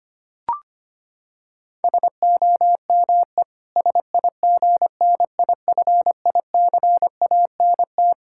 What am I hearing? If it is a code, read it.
Morse code, heard as a keyed tone, about 25 wpm, 700 Hz. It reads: SOME SIGNIFICANT
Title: Sentences from Top 1600 Words 25wpm